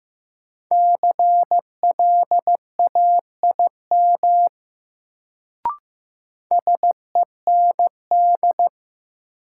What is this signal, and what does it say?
Morse code, heard as a keyed tone, about 15 wpm, 700 Hz. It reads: CLAIM SEND
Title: Sentences from Top 1500 Words - New Word Drill - 15wpm